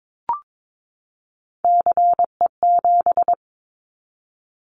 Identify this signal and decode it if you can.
Morse code, heard as a keyed tone, about 22 wpm, 700 Hz. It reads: CE7
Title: Callsign Prefixes for Top 100 DXCC - 22wpm